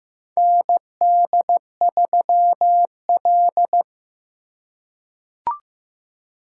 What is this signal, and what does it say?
Morse code, heard as a keyed tone, about 15 wpm, 700 Hz. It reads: ND3L
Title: K1USN SST QSO Practice - 15wpm